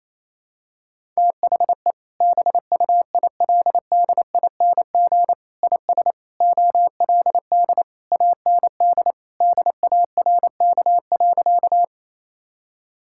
Morse code, heard as a keyed tone, about 27 wpm, 700 Hz. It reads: THE BUILDING IS OLD AND DARK.
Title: Sentences from Top 1000 Words - Review Part 5 - 28wpm